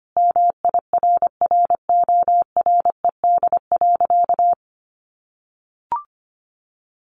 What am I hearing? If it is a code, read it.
Morse code, heard as a keyed tone, about 25 wpm, 700 Hz. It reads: MIRRORED.
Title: A Sound Of Thunder 25wpm